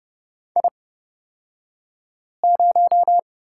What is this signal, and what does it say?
Morse code, heard as a keyed tone, about 30 wpm, spaced as for 5 wpm, 700 Hz. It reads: I0
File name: Lesson 36d - BT - Two Characters 30wpm